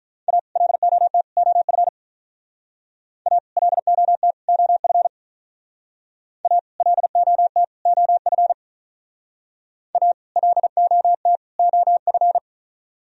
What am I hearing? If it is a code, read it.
Morse code, heard as a keyed tone, about 44 wpm, 700 Hz. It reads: A LOT OF A LOT OF A LOT OF A LOT OF